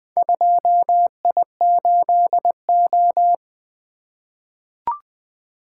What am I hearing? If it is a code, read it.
Morse code, heard as a keyed tone, about 20 wpm, 700 Hz. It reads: 2I8O
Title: Four Letter-Number Combinations - 20wpm